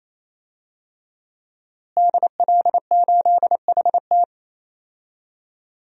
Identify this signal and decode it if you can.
Morse code, heard as a keyed tone, about 28 wpm, 700 Hz. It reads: DL8HT
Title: DXCC-100 Call Signs - 5 Characters - Repeated 3x - 28wpm